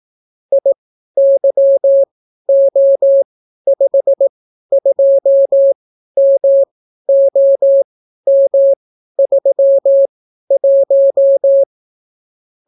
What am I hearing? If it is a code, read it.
Morse code, heard as a keyed tone, about 18 wpm, spaced as for 12 wpm, 550 Hz. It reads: IYO52MOM31